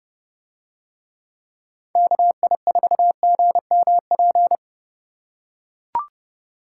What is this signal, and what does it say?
Morse code, heard as a keyed tone, about 29 wpm, 700 Hz. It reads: KI4GMP